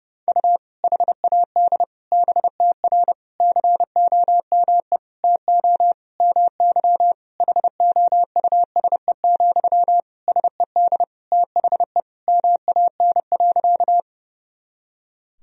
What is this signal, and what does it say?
Morse code, heard as a keyed tone, about 30 wpm, 700 Hz. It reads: U HAD BTR COME TO MY HOUSE, SED THE MAN.
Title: Sentences from Top 400 Words and Abbreviations 30wpm